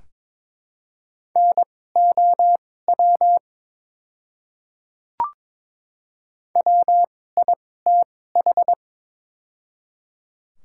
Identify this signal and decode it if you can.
Morse code, heard as a keyed tone, about 22 wpm, spaced as for 16 wpm, 700 Hz. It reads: NOW WITH
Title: 2x Character Spacing - Top 100 Words - 22wpm